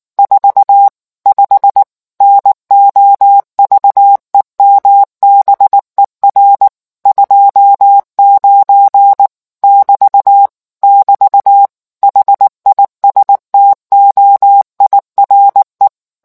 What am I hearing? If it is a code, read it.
Morse code, heard as a keyed tone, about 19 wpm, 800 Hz. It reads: 45 NOVEMBER 29 <BT> <BT> HISTOIRE